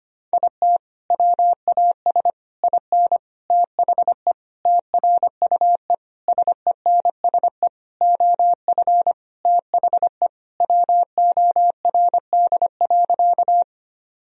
Morse code, heard as a keyed tone, about 25 wpm, 700 Hz. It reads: IT WAS IN THE TRUE SENSE OF THE WORD.
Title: Sentences from Top 1000 Words - Review Part 4 - 25wpm